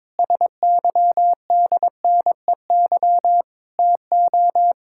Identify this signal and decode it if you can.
Morse code, heard as a keyed tone, about 22 wpm, 700 Hz. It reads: SYDNEY TO
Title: Pangram Challenge v2 - 22wpm